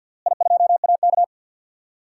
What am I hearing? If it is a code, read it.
Morse code, heard as a keyed tone, about 50 wpm, 700 Hz. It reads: I2AK
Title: DXCC-100 Call Signs - 4 Characters - Repeated 3x - 50wpm